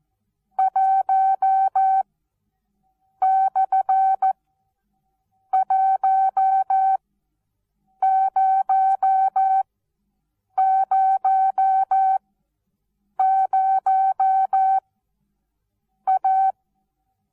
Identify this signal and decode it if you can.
Morse code, heard as a keyed tone, about 14 wpm, spaced as for 6 wpm, 770 Hz. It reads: 1/1000A
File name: Onlinelehrgang / morsen / DJ4UF Morsen19